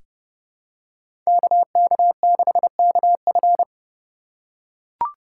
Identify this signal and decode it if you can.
Morse code, heard as a keyed tone, about 30 wpm, 700 Hz. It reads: KK6KF
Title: USA Call Signs - 5 Characters - 8x Character Spacing - 30wpm